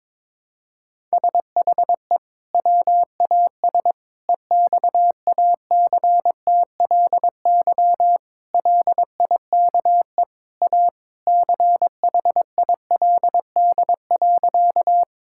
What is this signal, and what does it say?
Morse code, heard as a keyed tone, about 21 wpm, 700 Hz. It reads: SHE WAS EXACTLY LIKE A CHILD.
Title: Sentences from Top 1000 Words - Review Part 6 - 22wpm